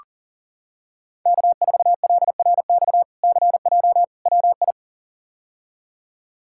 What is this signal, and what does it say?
Morse code, heard as a keyed tone, about 40 wpm, 700 Hz. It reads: K4LRX CJ WI
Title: K1USN SST QSO Practice - 40wpm